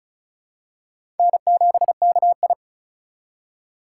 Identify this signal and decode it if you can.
Morse code, heard as a keyed tone, about 36 wpm, 700 Hz. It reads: N7KI
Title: USA Call Signs - 4 Characters - 4x Character Spacing - 35wpm